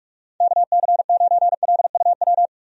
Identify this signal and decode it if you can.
Morse code, heard as a keyed tone, about 44 wpm, 700 Hz. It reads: KC9LUW
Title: USA Call Signs - 6 Characters - 45wpm